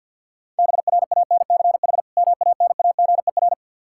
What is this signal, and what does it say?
Morse code, heard as a keyed tone, about 50 wpm, 700 Hz. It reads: BRANCH MANAGER
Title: Practice QSO - Exchange 3 - More Info - 50wpm